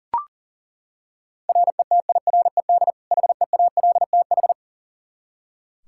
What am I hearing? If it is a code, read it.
Morse code, heard as a keyed tone, about 40 wpm, 700 Hz. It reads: RETIRED HEALTH
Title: Practice QSO - Exchange 3 - More Info - 40wpm